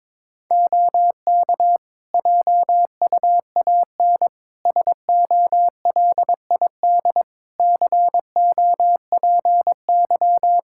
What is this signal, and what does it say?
Morse code, heard as a keyed tone, about 22 wpm, 700 Hz. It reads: OK JUAN SOLID COPY
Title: K1USN SST QSO Practice 22wpm